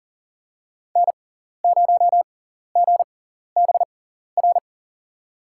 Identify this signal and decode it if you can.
Morse code, heard as a keyed tone, about 40 wpm, spaced as for 14 wpm, 700 Hz. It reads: N0GBR
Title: USA Call Signs - 6x Character Spacing - 40wpm